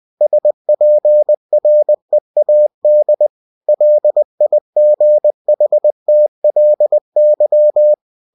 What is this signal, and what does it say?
Morse code, heard as a keyed tone, about 20 wpm, 600 Hz. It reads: SPREAD LIGHTLY